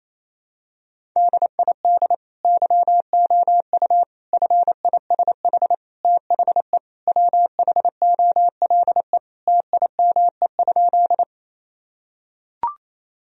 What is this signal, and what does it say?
Morse code, heard as a keyed tone, about 27 wpm, 700 Hz. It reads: DID YOU FISH THE WHOLE TIME?
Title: Sentences from Top 1000 Words - Review Part 1 - 28wpm